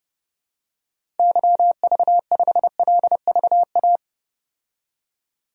Morse code, heard as a keyed tone, about 30 wpm, 700 Hz. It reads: YV5LVA
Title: DXCC-100 Call Signs - 6 Characters - 30wpm